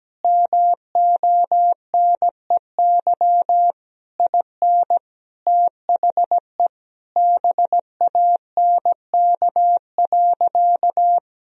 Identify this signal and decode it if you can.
Morse code, heard as a keyed tone, about 17 wpm, 700 Hz. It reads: MONEY IN THE BANK.
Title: Sentences from Top 1000 Words - Review Part 4 - 17wpm